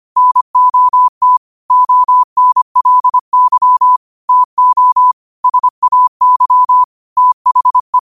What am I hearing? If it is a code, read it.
Morse code, heard as a keyed tone, about 25 wpm, 1000 Hz. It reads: NOT ONLY TO SAY THE
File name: qotdfiles / 25wpm